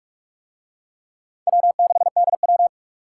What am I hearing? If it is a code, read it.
Morse code, heard as a keyed tone, about 40 wpm, 700 Hz. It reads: W6DW